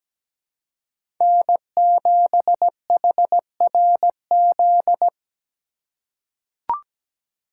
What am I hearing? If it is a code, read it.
Morse code, heard as a keyed tone, about 17 wpm, 700 Hz. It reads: N7HRZ
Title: USA Call Signs - 5 Characters - 2x Character Spacing - 17wpm